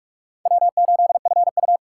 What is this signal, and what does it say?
Morse code, heard as a keyed tone, about 44 wpm, 700 Hz. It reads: W8FU